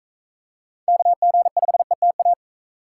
Morse code, heard as a keyed tone, about 44 wpm, 700 Hz. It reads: KG5ETA